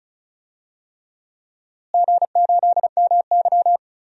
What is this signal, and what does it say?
Morse code, heard as a keyed tone, about 36 wpm, 700 Hz. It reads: G8MY
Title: DXCC-100 Call Signs - 4 Characters - Repeated 3x - 35wpm